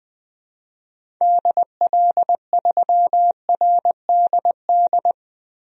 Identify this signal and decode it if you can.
Morse code, heard as a keyed tone, about 20 wpm, 700 Hz. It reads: DL3RDD
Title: DXCC-100 Call Signs - 6 Characters - Repeated 3x - 20wpm